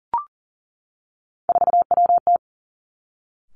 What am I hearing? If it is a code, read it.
Morse code, heard as a keyed tone, about 40 wpm, 700 Hz. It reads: 4WT
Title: Three Challenging Letter-Number Combinations 40wpm